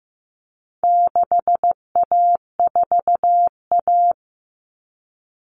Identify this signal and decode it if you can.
Morse code, heard as a keyed tone, about 15 wpm, 700 Hz. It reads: 6A4A
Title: Common Cut Numbers - 4-Digit 15wpm